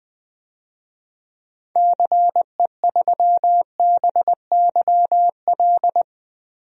Morse code, heard as a keyed tone, about 20 wpm, 700 Hz. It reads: CE3BYL
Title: DXCC-100 Call Signs - 6 Characters - Repeated 3x - 20wpm